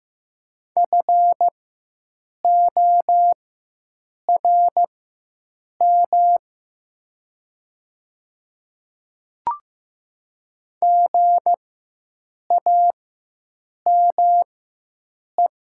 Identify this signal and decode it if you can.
Morse code, heard as a keyed tone, about 15 wpm, spaced as for 7 wpm, 700 Hz. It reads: FORM GAME
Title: 4x Character Spacing - Top 500 Words - 15wpm